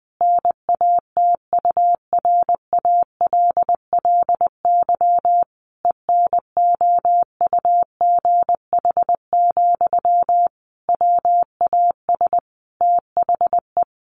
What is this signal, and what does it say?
Morse code, heard as a keyed tone, about 20 wpm, 700 Hz. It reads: NATURALLY ENOUGH, WAS THE
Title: Sentences from Top 1300 Words 20wpm